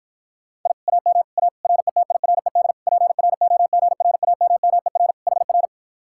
Morse code, heard as a keyed tone, about 53 wpm, 700 Hz. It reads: I AM A RETIRED PROGRAMMER HR